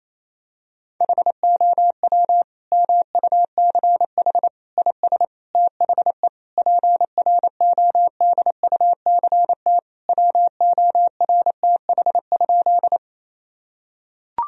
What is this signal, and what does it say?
Morse code, heard as a keyed tone, about 28 wpm, 700 Hz. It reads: HOW MUCH IS THE PRODUCT WORTH?
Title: Sentences from Top 1000 Words - Review Part 2 - 28wpm